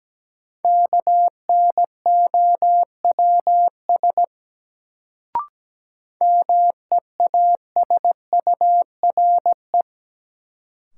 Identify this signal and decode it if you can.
Morse code, heard as a keyed tone, about 17 wpm, 700 Hz. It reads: KNOWS MEASURE
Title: Sentences from Top 1000 Words - Review Part 1 - New Word Drill - 17wpm